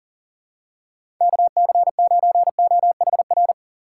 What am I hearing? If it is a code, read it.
Morse code, heard as a keyed tone, about 40 wpm, 700 Hz. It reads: KC9OHR
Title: USA Call Signs - 4x Character Spacing - 40wpm